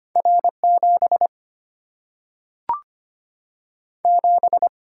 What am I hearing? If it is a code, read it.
Morse code, heard as a keyed tone, about 25 wpm, 690 Hz. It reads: R 7 7
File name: Two Letter-Number Combinations - 25wpm